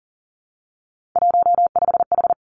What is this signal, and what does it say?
Morse code, heard as a keyed tone, about 39 wpm, 700 Hz. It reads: 15H
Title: Three Challenging Letter-Number Combinations 40wpm